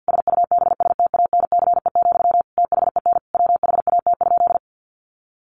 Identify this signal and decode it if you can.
Morse code, heard as a keyed tone, about 50 wpm, 700 Hz. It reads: SUBSTANCE, THEN WHAT?
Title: Sentences from Top 1500 Words 50wpm